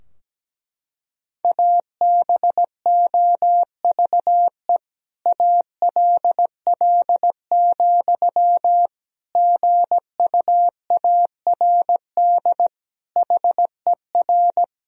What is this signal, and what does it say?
Morse code, heard as a keyed tone, about 17 wpm, 700 Hz. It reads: ABOVE ALL, GUARD HER
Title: Sentences from Top 1000 Words - Review Part 4 - 17wpm